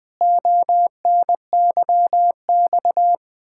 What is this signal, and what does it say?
Morse code, heard as a keyed tone, about 20 wpm, 700 Hz. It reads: ONYX
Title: Pangram Challenge v2 - 20wpm